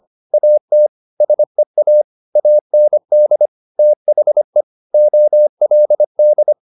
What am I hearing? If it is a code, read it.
Morse code, heard as a keyed tone, about 25 wpm, 600 Hz. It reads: AT SEA AND THE OLD
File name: the old man and the sea part 1 25wpm